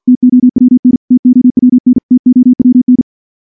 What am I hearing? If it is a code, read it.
Morse code, heard as a keyed tone, about 51 wpm, 250 Hz. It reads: TOWN TOWN TOWN
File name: morse / morse-training-50WPM